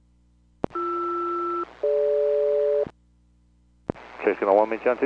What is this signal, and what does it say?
Aircraft SELCAL, heard as two pairs of tones, with speech. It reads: BR-DG